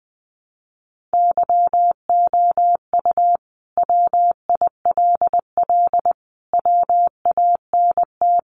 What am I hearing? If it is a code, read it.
Morse code, heard as a keyed tone, about 20 wpm, 700 Hz. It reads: YOU WILL WANT
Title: Sentences from Top 700 Words 20wpm